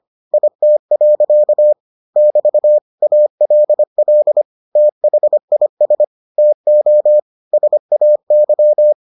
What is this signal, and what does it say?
Morse code, heard as a keyed tone, about 25 wpm, 600 Hz. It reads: IT. <BT> ALL THIS TO SAY